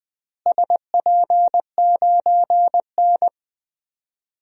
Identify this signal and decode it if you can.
Morse code, heard as a keyed tone, about 20 wpm, 700 Hz. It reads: SP9N
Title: DXCC-100 Call Signs - 4 Characters - Repeated 3x - 20wpm